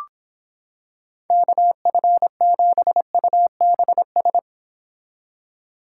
Morse code, heard as a keyed tone, about 26 wpm, 700 Hz. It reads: KF7UBS